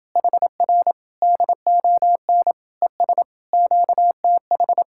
Morse code, heard as a keyed tone, about 30 wpm, 700 Hz. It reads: HR DON ES QTH